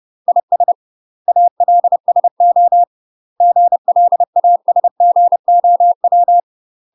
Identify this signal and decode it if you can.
Morse code, heard as a keyed tone, about 30 wpm, 700 Hz. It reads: IS ALSO GLASGOW
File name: UK MASTER QSOs TRAINING / 30WPM / UK MASTER QSOs 0046